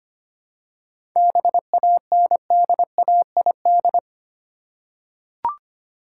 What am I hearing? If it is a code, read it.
Morse code, heard as a keyed tone, about 25 wpm, 700 Hz. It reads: BANDAID
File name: practice / 25wpm / Easy Words with Hints - 25wpm